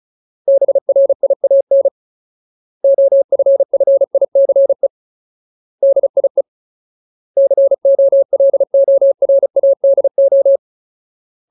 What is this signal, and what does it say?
Morse code, heard as a keyed tone, about 34 wpm, 550 Hz. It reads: BRIAN OFFICE DIE COLORADO
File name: Ultimate Word List / 35WPM Ultimate